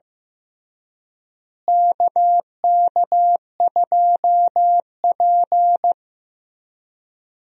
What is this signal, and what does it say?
Morse code, heard as a keyed tone, about 16 wpm, 700 Hz. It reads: KK2P
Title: USA Call Signs - 4 Characters - Repeated 3x - 15wpm